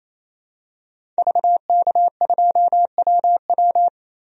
Morse code, heard as a keyed tone, about 28 wpm, 700 Hz. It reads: VK2WW
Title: DXCC-100 Call Signs - 5 Characters - 28wpm